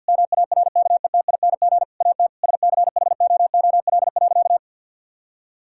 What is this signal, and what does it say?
Morse code, heard as a keyed tone, about 45 wpm, 700 Hz. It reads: MARKETING AT SCHOOL.
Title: Sentences from Top 1000 Words - Review Part 5 - 50wpm